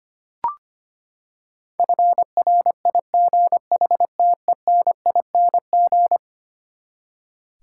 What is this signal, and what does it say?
Morse code, heard as a keyed tone, about 25 wpm, 700 Hz. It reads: FRIGHTENING
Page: morse-code-ninja-practice-25wpm